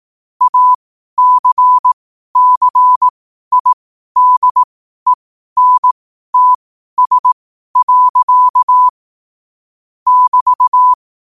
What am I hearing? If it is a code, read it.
Morse code, heard as a keyed tone, about 18 wpm, spaced as for 12 wpm, 1000 Hz. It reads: ACCIDENTS. <BT>
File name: qotdfiles / 12wpm